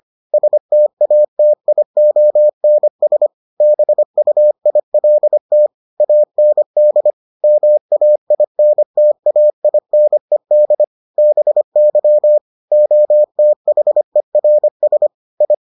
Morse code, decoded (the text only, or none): STATIONS BUILT AND MAINTAINED BY OTHERS I